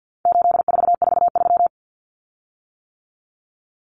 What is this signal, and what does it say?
Morse code, heard as a keyed tone, about 50 wpm, 700 Hz. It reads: 8443
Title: Common Cut Numbers - 4-Digit 50wpm